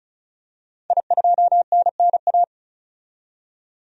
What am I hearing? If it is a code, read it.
Morse code, heard as a keyed tone, about 36 wpm, 700 Hz. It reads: I2NNA
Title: DXCC-100 Call Signs - 5 Characters - 35wpm